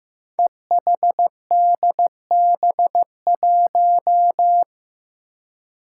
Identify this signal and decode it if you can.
Morse code, heard as a keyed tone, about 15 wpm, 700 Hz. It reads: EHDB1
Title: Five Challenging Letter-Number Combinations 15wpm